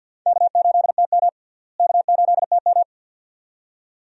Morse code, heard as a keyed tone, about 51 wpm, 700 Hz. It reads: K8TM K8TM
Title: USA Call Signs - 4 Characters - Repeated 2x - 50wpm